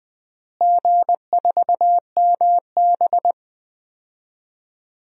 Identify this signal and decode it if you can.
Morse code, heard as a keyed tone, about 20 wpm, 700 Hz. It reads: G4MB
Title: DXCC-100 Call Signs - 4 Characters - Repeated 2x - 20wpm